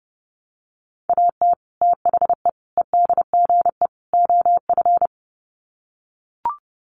Morse code, heard as a keyed tone, about 29 wpm, 700 Hz. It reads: AT THE EDGE OF
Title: Sets of 4 Words 30wpm